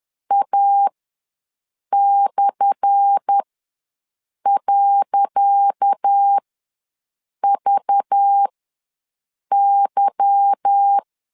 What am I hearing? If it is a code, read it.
Morse code, heard as a keyed tone, about 10 wpm, spaced as for 6 wpm, 800 Hz. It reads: A/.VY
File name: LEZIONI / 10 SIGNS / 10D COMA ws 10wpm 4314